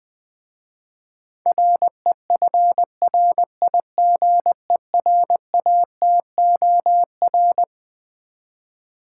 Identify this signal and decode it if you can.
Morse code, heard as a keyed tone, about 20 wpm, 700 Hz. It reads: REFRIGERATOR
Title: Easy Words with Hints - 20wpm